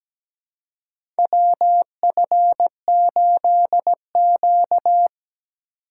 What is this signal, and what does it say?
Morse code, heard as a keyed tone, about 17 wpm, 700 Hz. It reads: WF8Q
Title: USA Call Signs - 4 Characters - 3x Character Spacing - 17wpm